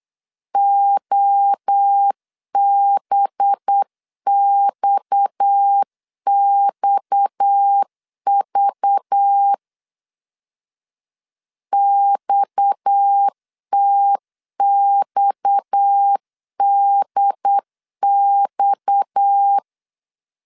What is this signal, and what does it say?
Morse code, heard as a keyed tone, about 8 wpm, 800 Hz. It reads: OBXXV XTXDX
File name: LEZIONI / 6 PKX0 / 6 PKX 8wpm 3041